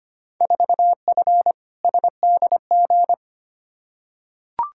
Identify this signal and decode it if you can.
Morse code, heard as a keyed tone, about 25 wpm, 700 Hz. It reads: 4F SDG
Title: Field Day Exchanges - 25wpm